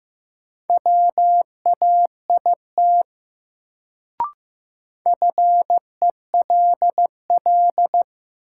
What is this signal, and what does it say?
Morse code, heard as a keyed tone, about 15 wpm, 700 Hz. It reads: WAIT FELL
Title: Sets of 5 Words Encore - New Word Drill - 15wpm